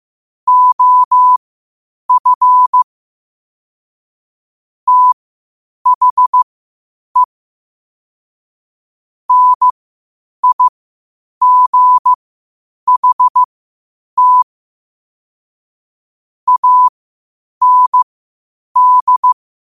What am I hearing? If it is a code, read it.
Morse code, heard as a keyed tone, about 15 wpm, spaced as for 8 wpm, 1000 Hz. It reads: OF THE NIGHT AND
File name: 8wpm